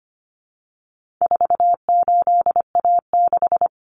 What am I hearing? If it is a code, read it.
Morse code, heard as a keyed tone, about 25 wpm, 700 Hz. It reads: 48A6
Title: Common Cut Numbers - 4-Digit 25wpm